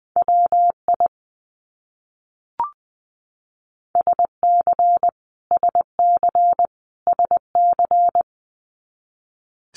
Morse code, heard as a keyed tone, about 20 wpm, 700 Hz. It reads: WI SC SC SC
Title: US States Repeated 3x 20wpm